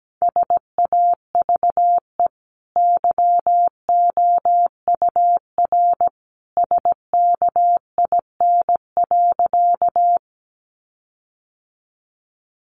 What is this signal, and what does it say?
Morse code, heard as a keyed tone, about 17 wpm, 700 Hz. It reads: SAVE YOUR SKIN.
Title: Sentences from Top 1300 Words 17wpm